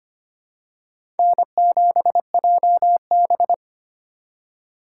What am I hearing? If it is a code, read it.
Morse code, heard as a keyed tone, about 25 wpm, 700 Hz. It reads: N7JB
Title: USA Call Signs - 4x Character Spacing - 25wpm